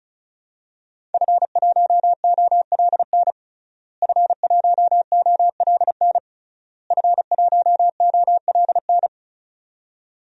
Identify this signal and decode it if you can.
Morse code, heard as a keyed tone, about 34 wpm, 700 Hz. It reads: F1OLN F1OLN F1OLN
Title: DXCC-100 Call Signs - 5 Characters - Repeated 3x - 35wpm